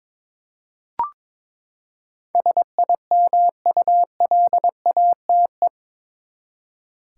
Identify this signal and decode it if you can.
Morse code, heard as a keyed tone, about 21 wpm, 700 Hz. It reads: SIMULATE
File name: practice / 22wpm / Eight Letter Words - 22wpm